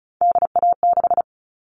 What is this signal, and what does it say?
Morse code, heard as a keyed tone, about 34 wpm, 700 Hz. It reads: DA6
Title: All Cut Numbers - 3-Digit 35wpm